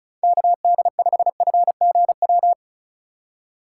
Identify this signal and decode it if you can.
Morse code, heard as a keyed tone, about 36 wpm, 700 Hz. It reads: KD5FGW
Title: USA Call Signs - 6 Characters - 2x Character Spacing - 35wpm